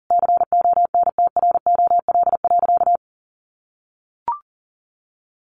Morse code, heard as a keyed tone, about 37 wpm, 700 Hz. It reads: CONTROL.
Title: Sentences from Top 1400 Words 40wpm